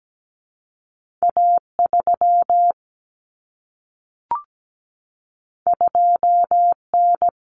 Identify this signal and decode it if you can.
Morse code, heard as a keyed tone, about 17 wpm, 700 Hz. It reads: A3 2N
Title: Common Cut Numbers - 2-Digit 17wpm